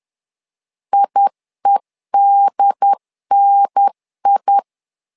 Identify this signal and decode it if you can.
Morse code, heard as a keyed tone, about 10 wpm, 800 Hz. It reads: IEDNI
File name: LEZIONI / 2 UHDN / 2 UHDN 10wpm 1904